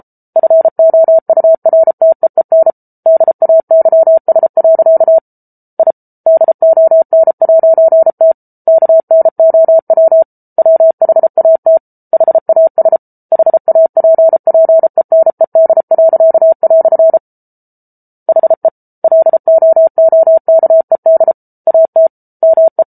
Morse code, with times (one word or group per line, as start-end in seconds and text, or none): 0.4-2.7 FOURTEEN
3.1-5.2 DAYS.
5.8-5.9 I
6.3-8.3 DON'T
8.7-10.2 KNOW
10.6-11.8 WHAT
12.1-13.0 HAS
13.3-17.2 HAPPENED."
18.3-18.7 HE
19.0-21.3 LOOKED
21.7-22.1 AT
22.4-22.8 ME